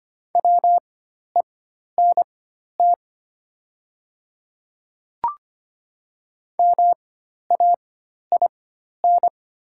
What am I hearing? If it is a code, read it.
Morse code, heard as a keyed tone, about 25 wpm, spaced as for 12 wpm, 700 Hz. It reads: WENT MAIN
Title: 4x Character Spacing - Top 500 Words - 25wpm